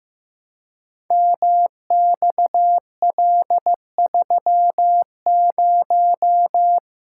Morse code, heard as a keyed tone, about 15 wpm, 700 Hz. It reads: MXL30
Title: Code Groups - 5 Letter:Numbers - 15wpm